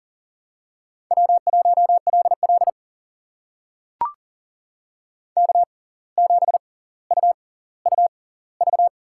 Morse code, heard as a keyed tone, about 38 wpm, spaced as for 19 wpm, 700 Hz. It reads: W1LL K7UUV
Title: USA Call Signs - 6x Character Spacing - 40wpm